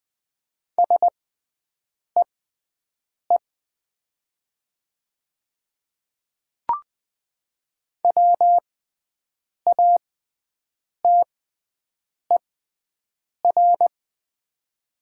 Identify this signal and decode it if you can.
Morse code, heard as a keyed tone, about 20 wpm, spaced as for 7 wpm, 700 Hz. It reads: SEE WATER
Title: 6x Character Spacing - Top 300 Words - 20wpm